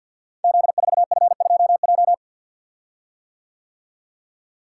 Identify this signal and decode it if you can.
Morse code, heard as a keyed tone, about 50 wpm, 700 Hz. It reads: Z4F2J
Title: Code Groups - 5 Letter:Numbers - 50wpm